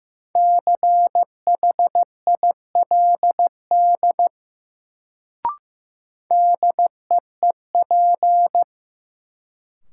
Morse code, heard as a keyed tone, about 15 wpm, 700 Hz. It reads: CHILD DEEP
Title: Sets of 5 Words Encore - New Word Drill - 15wpm